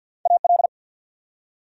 Morse code, heard as a keyed tone, about 53 wpm, 700 Hz. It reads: AL
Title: USA Common Names - Max 3 Letters - 50wpm